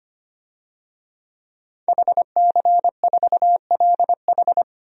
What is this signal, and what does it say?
Morse code, heard as a keyed tone, about 25 wpm, 700 Hz. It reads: HC4LH
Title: DXCC-100 Call Signs - 5 Characters - 25wpm